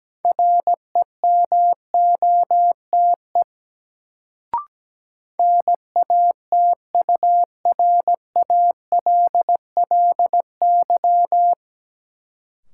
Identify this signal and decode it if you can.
Morse code, heard as a keyed tone, about 17 wpm, 700 Hz. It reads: REMOTE NATURALLY
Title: Sentences from Top 1300 Words - New Word Drill - 17wpm